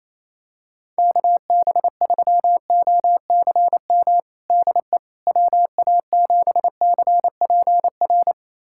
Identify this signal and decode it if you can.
Morse code, heard as a keyed tone, about 27 wpm, 700 Hz. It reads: KB3OCM DE WA7CPR
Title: USA Callsign DE Callsign 28wpm